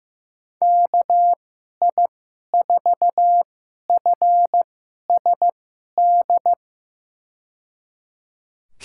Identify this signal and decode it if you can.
Morse code, heard as a keyed tone, about 15 wpm, spaced as for 11 wpm, 700 Hz. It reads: KI4FSD
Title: USA Call Signs - 6 Characters - 2x Character Spacing - 15wpm